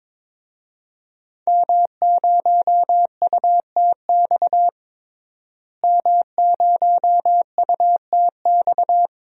Morse code, heard as a keyed tone, about 22 wpm, 700 Hz. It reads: M0UTX M0UTX
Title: DXCC-100 Call Signs - 5 Characters - Repeated 2x - 22wpm